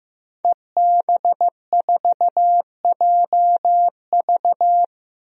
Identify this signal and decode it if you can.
Morse code, heard as a keyed tone, about 15 wpm, 700 Hz. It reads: EB4JV
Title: DXCC-100 Call Signs - 5 Characters - Repeated 3x - 15wpm